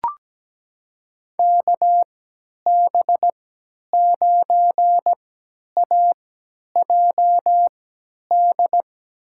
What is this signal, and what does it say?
Morse code, heard as a keyed tone, about 17 wpm, spaced as for 10 wpm, 700 Hz. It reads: KB9AJD
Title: USA Call Signs - 6 Characters - 3x Character Spacing - 17wpm